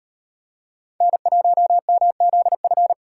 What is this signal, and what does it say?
Morse code, heard as a keyed tone, about 38 wpm, 700 Hz. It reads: N1MZF